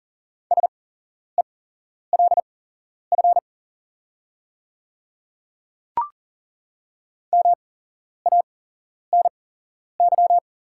Morse code, heard as a keyed tone, about 42 wpm, spaced as for 11 wpm, 700 Hz. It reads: SELF MANY
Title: 8x Character Spacing - Top 200 Words - 40wpm